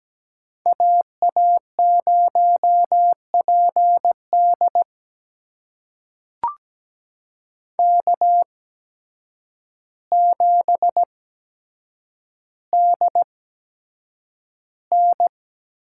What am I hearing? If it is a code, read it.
Morse code, heard as a keyed tone, about 17 wpm, spaced as for 8 wpm, 700 Hz. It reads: AA0PD K7DN
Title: USA Call Signs - 5 Characters - 8x Character Spacing - 17wpm